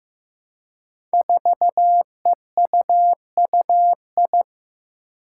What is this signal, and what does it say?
Morse code, heard as a keyed tone, about 15 wpm, 700 Hz. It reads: 4EUUI
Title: Five Challenging Letter-Number Combinations 15wpm